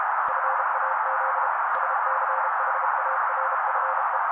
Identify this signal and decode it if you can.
Morse code, heard as a keyed tone, about 32 wpm, 530 Hz. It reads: FAZ SCHARFE
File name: daily / 31wpm noise